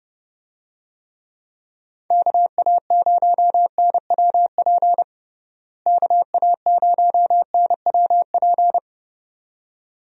Morse code, heard as a keyed tone, about 30 wpm, 700 Hz. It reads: KA0NWP KA0NWP
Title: USA Call Signs - 6 Characters - Repeated 2x - 30wpm